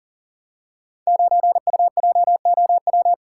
Morse code, heard as a keyed tone, about 40 wpm, 700 Hz. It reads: 9UJOW
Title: Code Groups - 5 Letter:Numbers - 40wpm